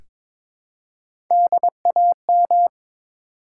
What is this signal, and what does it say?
Morse code, heard as a keyed tone, about 22 wpm, 700 Hz. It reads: DAM
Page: morse-code-ninja-practice-22wpm